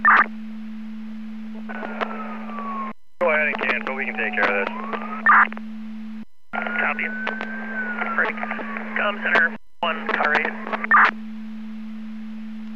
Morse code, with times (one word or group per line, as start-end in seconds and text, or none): none